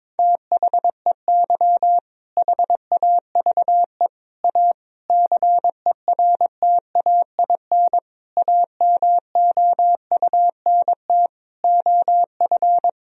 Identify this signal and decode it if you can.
Morse code, heard as a keyed tone, about 21 wpm, 700 Hz. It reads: THEY HAVE A CERTAIN AMOUNT OF
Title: Sentences from Top 1000 Words - Review Part 4 - 22wpm